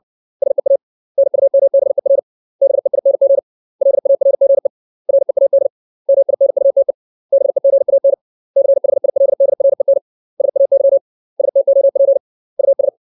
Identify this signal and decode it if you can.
Morse code, heard as a keyed tone, about 60 wpm, 550 Hz. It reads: SEA NUMBER BEING LARGE READ MINUTE BEGAN CHILDREN SAY STOP US
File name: 632 words / 632 Words 60WPM